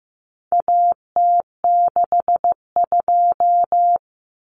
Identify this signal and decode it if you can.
Morse code, heard as a keyed tone, about 15 wpm, 700 Hz. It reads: AT62
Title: Common Cut Numbers - 4-Digit 15wpm